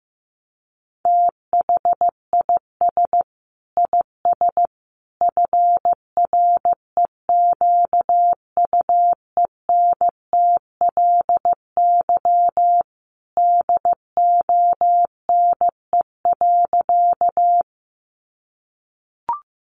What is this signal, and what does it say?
Morse code, heard as a keyed tone, about 15 wpm, 700 Hz. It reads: THIS IS FREQUENTLY DONE.
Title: Sentences from Top 1300 Words 15wpm